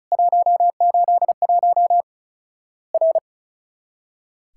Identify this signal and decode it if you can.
Morse code, heard as a keyed tone, about 36 wpm, 700 Hz. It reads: 181 R
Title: Worked All Europe DX Contest - QTC Exchange - 35wpm